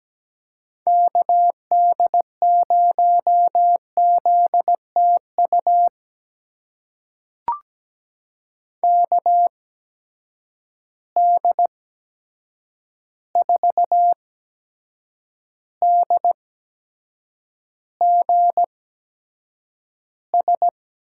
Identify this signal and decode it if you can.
Morse code, heard as a keyed tone, about 17 wpm, spaced as for 7 wpm, 700 Hz. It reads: KD0ZTU KD4DGS
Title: USA Call Signs - 6 Characters - 8x Character Spacing - 17wpm